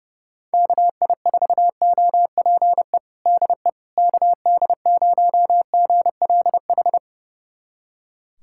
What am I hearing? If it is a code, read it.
Morse code, heard as a keyed tone, about 30 wpm, 700 Hz. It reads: KI4OPE DE KD0GLH
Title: USA Callsign DE Callsign 30wpm